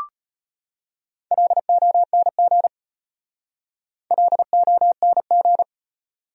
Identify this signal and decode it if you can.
Morse code, heard as a keyed tone, about 33 wpm, 700 Hz. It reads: LONG LONG